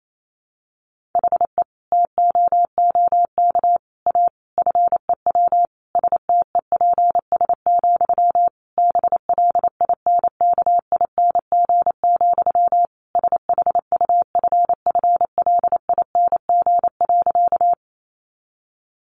Morse code, heard as a keyed tone, about 28 wpm, 700 Hz. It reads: HE TOOK A FEW STEPS, BLINKING, SHUFFLING.
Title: A Sound Of Thunder 28wpm